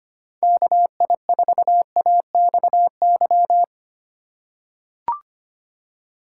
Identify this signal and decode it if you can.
Morse code, heard as a keyed tone, about 25 wpm, 700 Hz. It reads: KI4AXY
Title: USA Call Signs - 6 Characters - 3x Character Spacing - 25wpm